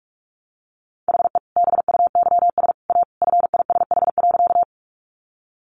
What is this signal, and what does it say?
Morse code, heard as a keyed tone, about 44 wpm, 700 Hz. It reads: HE BUYS A FISH.
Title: Additional Easy Phrases 45wpm